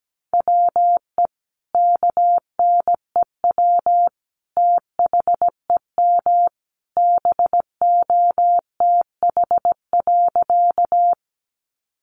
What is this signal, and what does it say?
Morse code, heard as a keyed tone, about 17 wpm, 700 Hz. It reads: WE KNEW THEM BOTH.
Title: Sentences from Top 400 Words 17wpm